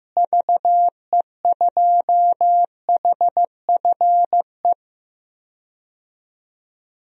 Morse code, heard as a keyed tone, about 16 wpm, 700 Hz. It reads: VE2HFE
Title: DXCC-100 Call Signs - 6 Characters - Repeated 2x - 15wpm